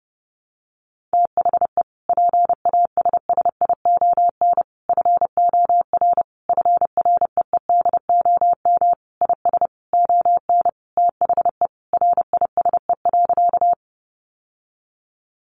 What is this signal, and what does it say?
Morse code, heard as a keyed tone, about 30 wpm, 700 Hz. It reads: THE PASSION FOR FREEDOM IS ON THE RISE.
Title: Sentences from Top 1300 Words 30wpm